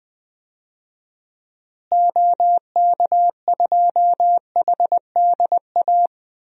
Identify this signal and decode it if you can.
Morse code, heard as a keyed tone, about 21 wpm, 700 Hz. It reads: OK2HDA
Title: DXCC-100 Call Signs - 6 Characters - Repeated 3x - 20wpm